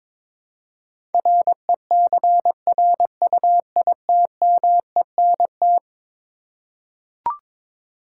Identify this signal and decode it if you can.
Morse code, heard as a keyed tone, about 21 wpm, 700 Hz. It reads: RECRUITMENT
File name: practice / 22wpm / Eleven Letter Words - 22wpm